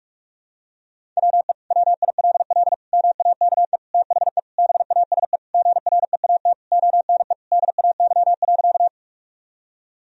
Morse code, heard as a keyed tone, about 46 wpm, 700 Hz. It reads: WE WILL MAKE THE BASE GREAT ONE DAY.
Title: Sentences from Top 1000 Words - Review Part 1 - 45wpm